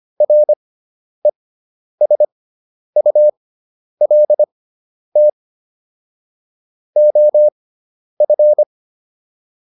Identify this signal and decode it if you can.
Morse code, heard as a keyed tone, about 25 wpm, spaced as for 10 wpm, 600 Hz. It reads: RESULT OF
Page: intermediate-practice-files